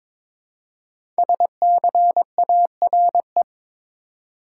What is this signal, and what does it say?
Morse code, heard as a keyed tone, about 22 wpm, 700 Hz. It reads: SCARE
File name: 5 Letter Words - First 3 Letters Given - 22wpm